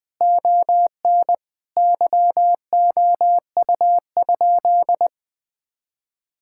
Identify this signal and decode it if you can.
Morse code, heard as a keyed tone, about 21 wpm, 700 Hz. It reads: ON YOU?
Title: Sentences from Top 1000 Words - Review Part 5 - 20wpm